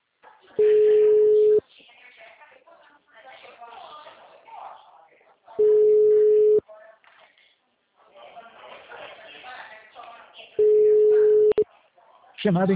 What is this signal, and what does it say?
Ringback tone (Europe, 425 Hz), with speech.
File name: out- -2019- - - .1997